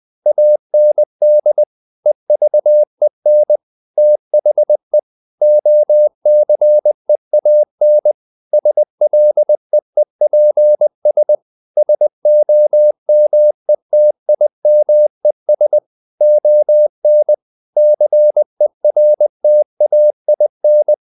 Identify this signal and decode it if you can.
Morse code, heard as a keyed tone, about 20 wpm, 600 Hz. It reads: AND EVEN THE OCEAN SLEEPS SOMETIMES ON CERTAIN
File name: the old man and the sea part 2 20wpm